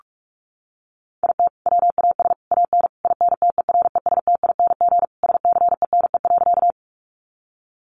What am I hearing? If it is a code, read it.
Morse code, heard as a keyed tone, about 44 wpm, 700 Hz. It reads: IT WAS AN INTERESTING SCENE.
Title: Sentences from Top 1100 Words 45wpm